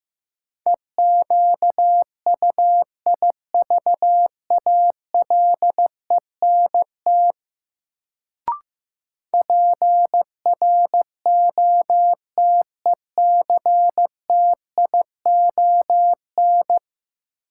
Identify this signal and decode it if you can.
Morse code, heard as a keyed tone, about 15 wpm, 700 Hz. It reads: EQUIVALENT PROTECTION
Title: Sentences from Top 1600 Words - New Word Drill Alt - 15wpm